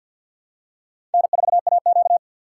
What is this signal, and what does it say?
Morse code, heard as a keyed tone, about 49 wpm, 700 Hz. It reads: N4AQ
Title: USA Call Signs - 4 Characters - 2x Character Spacing - 50wpm